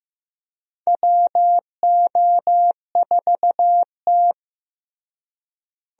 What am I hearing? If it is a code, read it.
Morse code, heard as a keyed tone, about 15 wpm, 700 Hz. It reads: WO4T
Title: USA Call Signs - 4 Characters - Repeated 3x - 15wpm